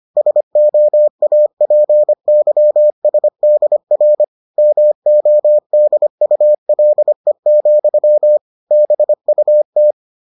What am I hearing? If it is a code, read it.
Morse code, heard as a keyed tone, about 25 wpm, 600 Hz. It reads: SOAPYSDR MODULE, BUT